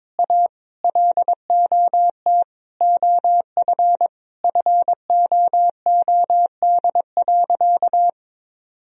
Morse code, heard as a keyed tone, about 21 wpm, 700 Hz. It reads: A LOT OF FOOD.
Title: Sentences from Top 1000 Words - Review Part 3 - 22wpm